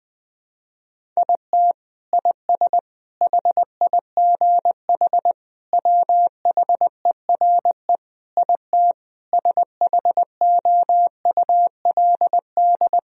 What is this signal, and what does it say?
Morse code, heard as a keyed tone, about 20 wpm, 700 Hz. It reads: IT IS HIGH WHERE IT SHOULD
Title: Sentences from Top 1000 Words - Review Part 1 - 20wpm